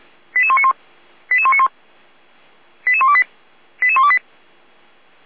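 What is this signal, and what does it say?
Five-tone selective call (ZVEI), heard as quick runs of five tones, tones of about 70 ms. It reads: 88191 88191 88128 88128
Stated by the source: ZVEI five-tone selective call received by SDR, from the Feuerwehr Orlamünde audio archive.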